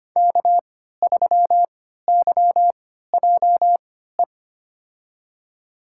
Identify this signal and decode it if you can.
Morse code, heard as a keyed tone, about 25 wpm, spaced as for 14 wpm, 700 Hz. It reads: K3YJE
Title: USA Call Signs - 5 Characters - 3x Character Spacing - 25wpm